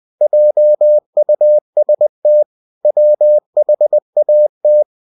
Morse code, heard as a keyed tone, about 20 wpm, 600 Hz. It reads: JUST WHAT
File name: cw-books / winnie-the-pooh / winnie-20wpm-page-0049